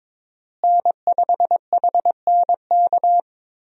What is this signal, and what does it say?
Morse code, heard as a keyed tone, about 22 wpm, 700 Hz. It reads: N5HNK
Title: USA Call Signs - 5 Characters - Repeated 2x - 22wpm